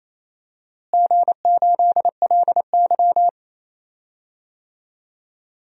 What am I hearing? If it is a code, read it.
Morse code, heard as a keyed tone, about 27 wpm, 700 Hz. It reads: G8LY
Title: DXCC-100 Call Signs - 4 Characters - Repeated 2x - 28wpm